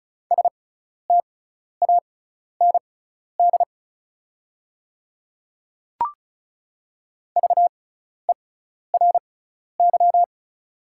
Morse code, heard as a keyed tone, about 36 wpm, spaced as for 12 wpm, 690 Hz. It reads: STAND VERY
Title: 6x Character Spacing - Top 300 Words - 35wpm